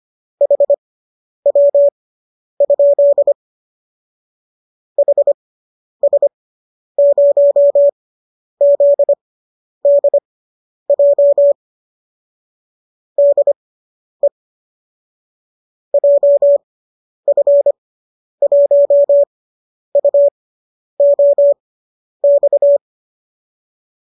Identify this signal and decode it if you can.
Morse code, heard as a keyed tone, about 25 wpm, spaced as for 10 wpm, 570 Hz. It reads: HW? HS0ZDJ DE JF1UOX